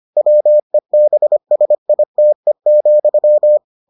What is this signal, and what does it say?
Morse code, heard as a keyed tone, about 25 wpm, 600 Hz. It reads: WEBSITE,